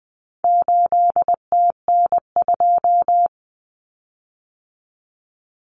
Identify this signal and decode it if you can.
Morse code, heard as a keyed tone, about 20 wpm, 700 Hz. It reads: 8TN2
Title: Common Cut Numbers - 4-Digit 20wpm